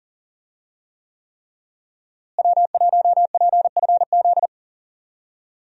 Morse code, heard as a keyed tone, about 40 wpm, 700 Hz. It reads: W1PFZ
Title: USA Call Signs - 5 Characters - Repeated 3x - 40wpm